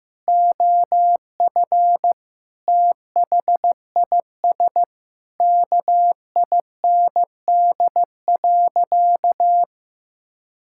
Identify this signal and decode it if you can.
Morse code, heard as a keyed tone, about 15 wpm, 700 Hz. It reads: OF THIS KIND.
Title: Sentences from Top 1000 Words - Review Part 5 - 15wpm